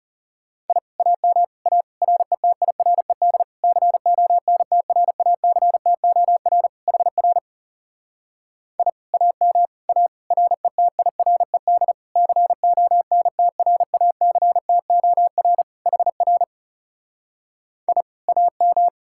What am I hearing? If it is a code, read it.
Morse code, heard as a keyed tone, about 40 wpm, 700 Hz. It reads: I AM A RETIRED CONTRACTOR HR I AM A RETIRED CONTRACTOR HR I AM